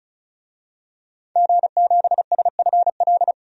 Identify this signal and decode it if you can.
Morse code, heard as a keyed tone, about 34 wpm, 700 Hz. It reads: G7SFL
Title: DXCC-100 Call Signs - 5 Characters - 35wpm